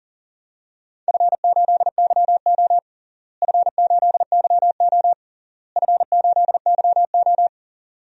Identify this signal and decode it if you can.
Morse code, heard as a keyed tone, about 40 wpm, 700 Hz. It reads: F8YO F8YO F8YO
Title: DXCC-100 Call Signs - 4 Characters - Repeated 3x - 40wpm